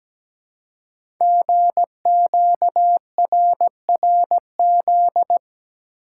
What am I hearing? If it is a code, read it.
Morse code, heard as a keyed tone, about 17 wpm, 700 Hz. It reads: GQRRZ
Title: Code Groups - 5 Letters - 17wpm